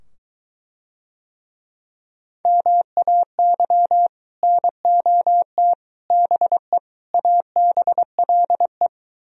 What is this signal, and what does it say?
Morse code, heard as a keyed tone, about 22 wpm, 700 Hz. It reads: MAY NOT BE ABLE